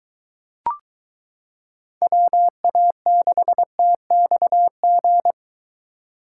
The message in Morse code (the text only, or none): WA6TXG